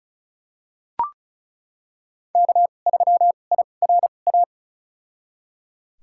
Morse code, heard as a keyed tone, about 36 wpm, spaced as for 25 wpm, 700 Hz. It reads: K3IRA
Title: USA Call Signs - 5 Characters - 2x Character Spacing - 35wpm